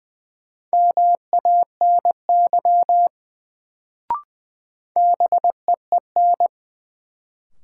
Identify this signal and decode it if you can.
Morse code, heard as a keyed tone, about 21 wpm, 700 Hz. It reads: MANY BEEN
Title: Sets of 2 Words - New Word Drill - 20wpm